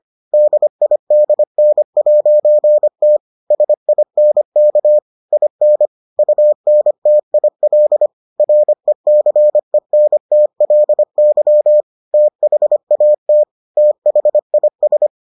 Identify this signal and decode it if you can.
Morse code, heard as a keyed tone, about 25 wpm, 600 Hz. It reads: DIDN'T SINK IN UNTIL RECENTLY THAT THIS